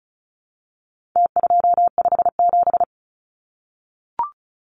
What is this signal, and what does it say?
Morse code, heard as a keyed tone, about 34 wpm, 700 Hz. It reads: T257
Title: Numbers - Contest Style - 4-Digit 35wpm